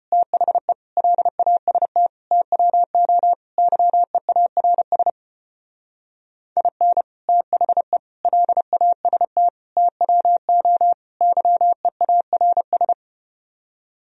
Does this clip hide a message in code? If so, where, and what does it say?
Yes: Morse code from 0.1 to 12.9 s, heard as a keyed tone, about 33 wpm, 700 Hz. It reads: THE LAST TWO YEARS IN THE LAST TWO YEARS